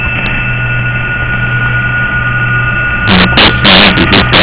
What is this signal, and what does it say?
PSK31, about 31.25 baud.